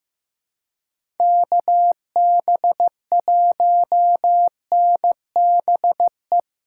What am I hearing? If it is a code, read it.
Morse code, heard as a keyed tone, about 15 wpm, 700 Hz. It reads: KB1NBE
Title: USA Callsign DE Callsign - 6 Characters - 15wpm